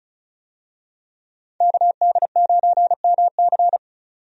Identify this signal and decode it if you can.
Morse code, heard as a keyed tone, about 36 wpm, 700 Hz. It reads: KD9MC
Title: USA Call Signs - 5 Characters - 35wpm